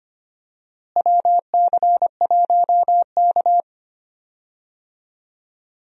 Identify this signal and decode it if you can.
Morse code, heard as a keyed tone, about 25 wpm, 700 Hz. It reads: WC1K
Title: USA Call Signs - 4 Characters - Repeated 2x - 25wpm